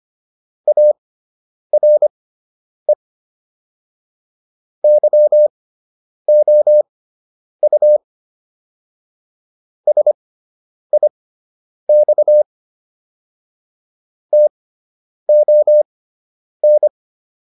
Morse code, heard as a keyed tone, about 25 wpm, spaced as for 9 wpm, 600 Hz. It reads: ARE YOU SIX TON